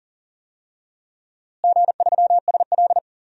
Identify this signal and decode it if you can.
Morse code, heard as a keyed tone, about 40 wpm, 700 Hz. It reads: G3SL
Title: DXCC-100 Call Signs - 4 Characters - Repeated 2x - 40wpm